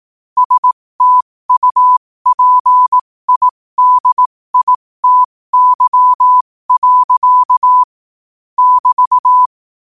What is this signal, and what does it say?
Morse code, heard as a keyed tone, about 18 wpm, spaced as for 15 wpm, 1000 Hz. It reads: STUPIDITY. <BT>
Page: New quotes every day in morse code at 15 Words per minute.